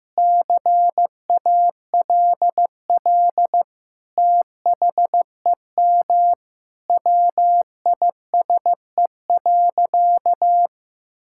Morse code, heard as a keyed tone, about 16 wpm, 700 Hz. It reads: CALL THEM WISE.
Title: Sentences from Top 1600 Words 15wpm